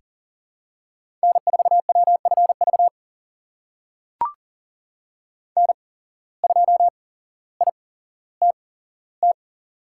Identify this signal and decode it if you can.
Morse code, heard as a keyed tone, about 38 wpm, spaced as for 17 wpm, 700 Hz. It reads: N4WFV N2ITT
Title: USA Call Signs - 5 Characters - 8x Character Spacing - 40wpm